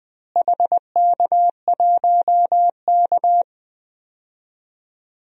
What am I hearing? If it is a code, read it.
Morse code, heard as a keyed tone, about 21 wpm, 700 Hz. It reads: HK1K
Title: DXCC-100 Call Signs - 4 Characters - Repeated 3x - 20wpm